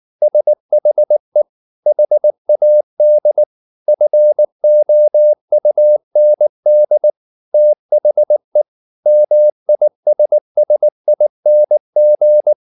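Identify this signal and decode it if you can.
Morse code, heard as a keyed tone, about 19 wpm, 600 Hz. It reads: SHE HAD FOUND THE MISSING
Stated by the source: Morse code at 19 wpm from CW med Gnister, reading Ken Follett's novel Never in English.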